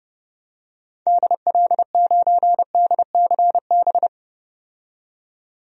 Morse code, heard as a keyed tone, about 30 wpm, 700 Hz. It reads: DL9DCB
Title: DXCC-100 Call Signs - 6 Characters - 30wpm